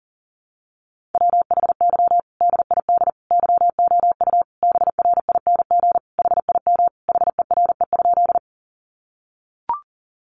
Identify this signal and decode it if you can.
Morse code, heard as a keyed tone, about 40 wpm, 700 Hz. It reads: WHY DID YOU BRING HIM HERE?
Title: Sentences from Top 500 Words 40wpm